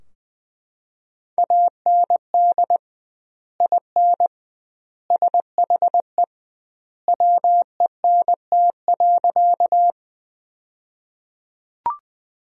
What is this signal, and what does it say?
Morse code, heard as a keyed tone, about 20 wpm, 700 Hz. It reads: AND IN SHE WENT.
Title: 2x Word-Spacing - Sentences from Top 200 Words - 20wpm